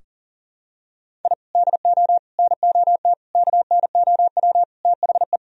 Morse code, heard as a keyed tone, about 40 wpm, 700 Hz. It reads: I DO NOT KNOW THE
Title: Sentences from Top 1000 Words - Review Part 3 - 40wpm